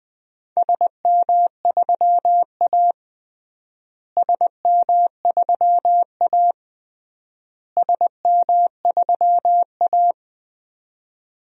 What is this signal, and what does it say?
Morse code, heard as a keyed tone, about 21 wpm, 700 Hz. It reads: SM3A SM3A SM3A
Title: DXCC-100 Call Signs - 4 Characters - Repeated 3x - 20wpm